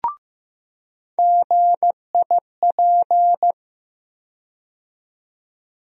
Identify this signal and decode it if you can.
Morse code, heard as a keyed tone, about 15 wpm, 700 Hz. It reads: GIP
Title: USA Call Sign Postfixes - 3 Letters 15wpm